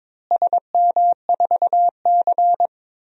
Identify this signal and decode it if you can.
Morse code, heard as a keyed tone, about 22 wpm, 700 Hz. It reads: SM4C
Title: DXCC-100 Call Signs - 4 Characters - Repeated 2x - 22wpm